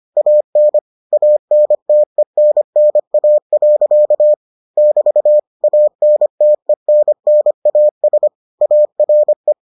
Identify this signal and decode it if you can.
Morse code, heard as a keyed tone, about 25 wpm, 600 Hz. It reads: AN ANTENNA. <BT> ANTENNAS ARE